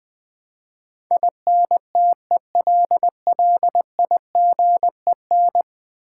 Morse code, heard as a keyed tone, about 20 wpm, 700 Hz. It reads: INTELLIGEN
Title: Build A Word - 5 Letters At A Time - 12-Letter Words - 20wpm